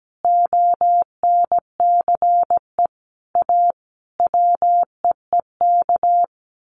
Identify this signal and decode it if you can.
Morse code, heard as a keyed tone, about 17 wpm, 700 Hz. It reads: ONCE A WEEK
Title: Sets of 5 Words Encore2 17wpm